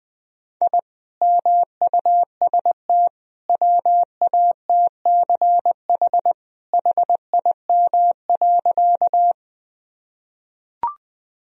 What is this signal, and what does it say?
Morse code, heard as a keyed tone, about 20 wpm, 700 Hz. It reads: I MUST WATCH HIM.
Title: Sentences from Top 1000 Words - Review Part 1 - 20wpm